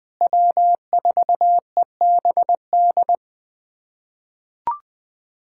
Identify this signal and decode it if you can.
Morse code, heard as a keyed tone, about 20 wpm, 700 Hz. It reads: W4EBD
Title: USA Callsign DE Callsign 20wpm